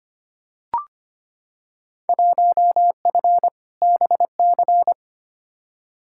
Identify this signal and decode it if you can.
Morse code, heard as a keyed tone, about 25 wpm, 700 Hz. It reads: 1F BC
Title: Field Day Exchanges Encore - 25wpm